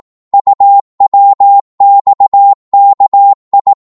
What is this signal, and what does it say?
Morse code, heard as a keyed tone, about 18 wpm, 800 Hz. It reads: UWXKI
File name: Morse / ltr-18wpm